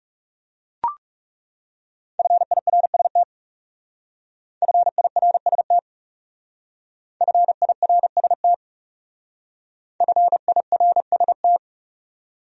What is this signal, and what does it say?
Morse code, heard as a keyed tone, about 36 wpm, 700 Hz. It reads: FIRST FIRST FIRST FIRST
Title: Speed-Racing Top 500 Words 30wpm